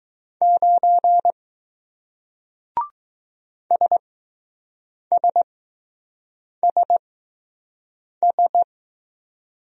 Morse code, heard as a keyed tone, about 20 wpm, spaced as for 6 wpm, 700 Hz. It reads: 9 SSSS